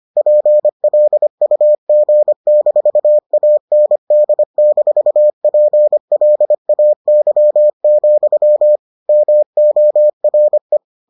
Morse code, heard as a keyed tone, about 25 wpm, 600 Hz. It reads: PLUG-AND-PLAY, MORE